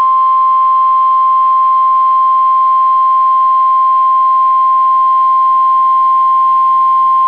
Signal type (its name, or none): Emergency Alert System (SAME) tones